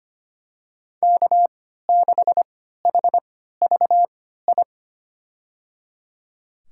Morse code, heard as a keyed tone, about 25 wpm, spaced as for 14 wpm, 700 Hz. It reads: K6HVI
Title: USA Call Signs - 5 Characters - 3x Character Spacing - 25wpm